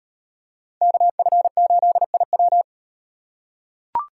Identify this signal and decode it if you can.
Morse code, heard as a keyed tone, about 35 wpm, 700 Hz. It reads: KF8IW